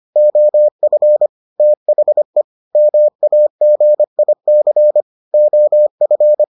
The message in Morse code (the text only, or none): OF THE MAGIC OF